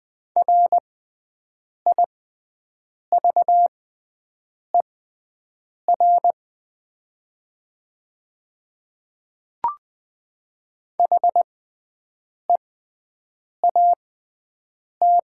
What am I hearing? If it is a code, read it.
Morse code, heard as a keyed tone, about 20 wpm, spaced as for 7 wpm, 700 Hz. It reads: RIVER HEAT
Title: 6x Character Spacing - Top 500 Words - 20wpm